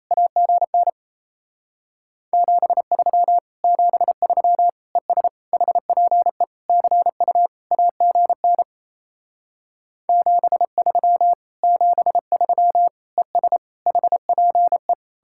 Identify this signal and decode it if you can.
Morse code, heard as a keyed tone, about 36 wpm, 700 Hz. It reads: AGN 73 73 ES HPE CU AGN 73 73 ES HPE